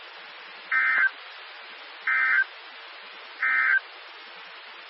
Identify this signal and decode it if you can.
Emergency Alert System (SAME) tones, heard as harsh buzzing bursts. It is heard from 0.6 to 3.9 s.
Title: Nassau and Suffolk County Amateur Repeaters